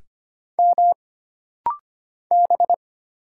Morse code, heard as a keyed tone, about 25 wpm, 700 Hz. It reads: M B
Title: Single Letter-Number Alt - Rapid-Fire 25wpm